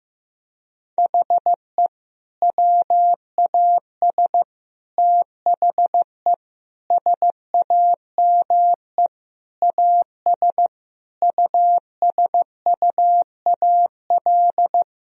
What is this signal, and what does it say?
Morse code, heard as a keyed tone, about 16 wpm, 700 Hz. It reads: HE WAS THE SAME AS USUAL
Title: Sentences from Top 1000 Words - Review Part 2 - 15wpm